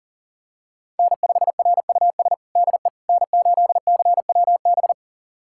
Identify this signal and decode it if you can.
Morse code, heard as a keyed tone, about 40 wpm, 700 Hz. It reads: N5RUS DE N8CWB
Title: USA Callsign DE Callsign - 5 Characters - 40wpm